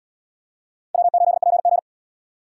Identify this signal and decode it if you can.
Morse code, heard as a keyed tone, about 69 wpm, 700 Hz. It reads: W8PG